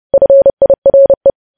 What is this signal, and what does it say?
Morse code, heard as a keyed tone, about 30 wpm, 550 Hz. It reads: FIRE
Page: news headlines in morse code at 30 wpm